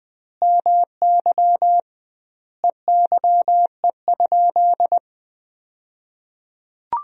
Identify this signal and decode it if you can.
Morse code, heard as a keyed tone, about 20 wpm, 700 Hz. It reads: MY EYE?
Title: 2x Word-Spacing - Sentences from Top 500 Words - 20wpm